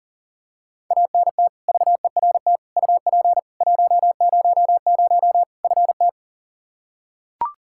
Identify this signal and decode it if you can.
Morse code, heard as a keyed tone, about 40 wpm, 700 Hz. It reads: ANT VERT UP 100 FT
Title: Practice QSO - Exchange 2 - Info - 40wpm